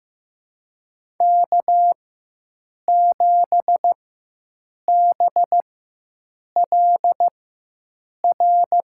Morse code, heard as a keyed tone, about 15 wpm, spaced as for 7 wpm, 700 Hz. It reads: K7BLR
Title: USA Call Signs - 5 Characters - 4x Character Spacing - 15wpm